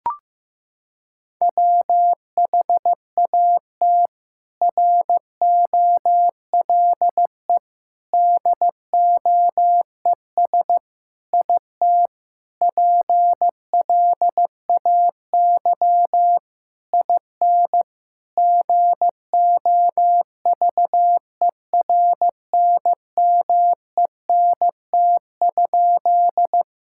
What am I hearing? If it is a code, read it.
Morse code, heard as a keyed tone, about 16 wpm, 700 Hz. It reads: WHAT ROLE DOES IT PLAY IN GOVERNMENT?
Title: Sentences from Top 1000 Words - Review Part 6 - 15wpm